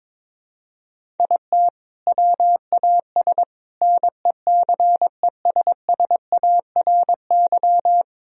Morse code, heard as a keyed tone, about 22 wpm, 700 Hz. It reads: IT WAS NECESSARY
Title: Sentences from Top 1000 Words - Review Part 1 - 22wpm